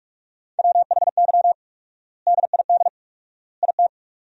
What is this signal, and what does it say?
Morse code, heard as a keyed tone, about 41 wpm, 700 Hz. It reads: WHY DID IT
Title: 4x Word-Spacing - Sentences from Top 400 Words - 45wpm